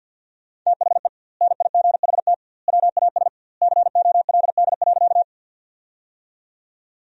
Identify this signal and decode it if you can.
Morse code, heard as a keyed tone, about 50 wpm, 700 Hz. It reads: THE NIGHT WAS COLD.
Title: 2x Word-Spacing - Sentences from Top 400 Words - 50wpm